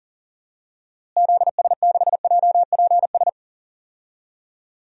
Morse code, heard as a keyed tone, about 40 wpm, 700 Hz. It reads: ZS6JPS
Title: DXCC-100 Call Signs - 6 Characters - Repeated 3x - 40wpm